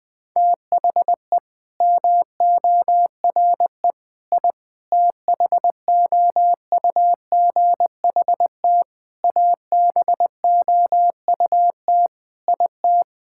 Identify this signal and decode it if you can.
Morse code, heard as a keyed tone, about 20 wpm, 700 Hz. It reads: THE MORE I THOUGHT ABOUT IT
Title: Sets of 8 Words - 20wpm